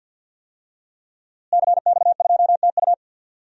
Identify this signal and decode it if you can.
Morse code, heard as a keyed tone, about 47 wpm, 700 Hz. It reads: CX2TU
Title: DXCC-100 Call Signs - 5 Characters - Repeated 2x - 50wpm